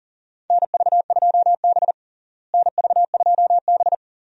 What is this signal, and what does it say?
Morse code, heard as a keyed tone, about 40 wpm, 700 Hz. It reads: NV2B NV2B
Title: USA Call Signs - 4 Characters - Repeated 2x - 40wpm